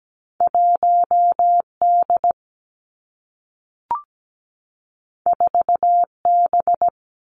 Morse code, heard as a keyed tone, about 17 wpm, 700 Hz. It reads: 1D 4B
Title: Two Challenging Letter-Number Combinations 17wpm